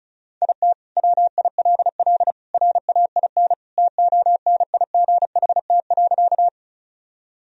Morse code, heard as a keyed tone, about 36 wpm, 700 Hz. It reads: IT WILL RAIN TONIGHT.
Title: Sentences from Top 1000 Words - Review Part 1 - 35wpm